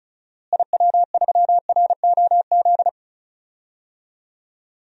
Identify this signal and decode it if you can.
Morse code, heard as a keyed tone, about 36 wpm, 700 Hz. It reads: IW3ROZ
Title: DXCC-100 Call Signs - 6 Characters - Repeated 2x - 35wpm